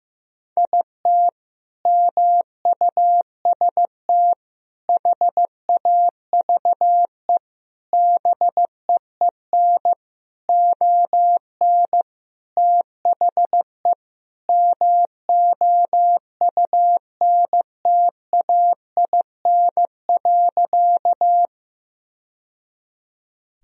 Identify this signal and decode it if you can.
Morse code, heard as a keyed tone, about 16 wpm, 700 Hz. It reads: IT MUST HAVE BEEN ON THE MOUNTAIN.
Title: Sentences from Top 1000 Words - Review Part 1 - 15wpm